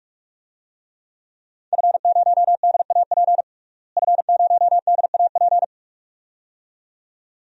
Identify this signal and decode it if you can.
Morse code, heard as a keyed tone, about 41 wpm, 700 Hz. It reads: F0DAP F0DAP
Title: DXCC-100 Call Signs - 5 Characters - Repeated 2x - 45wpm